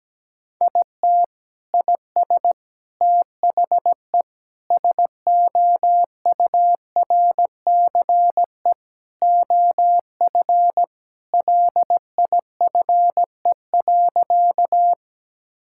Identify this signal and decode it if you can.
Morse code, heard as a keyed tone, about 17 wpm, 700 Hz. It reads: IT IS THE SOURCE OF LIFE.
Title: Sentences from Top 1000 Words - Review Part 5 - 17wpm